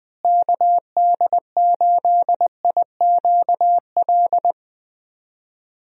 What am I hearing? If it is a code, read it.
Morse code, heard as a keyed tone, about 21 wpm, 700 Hz. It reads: KD8IQL
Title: USA Call Signs - 6 Characters - Repeated 3x - 20wpm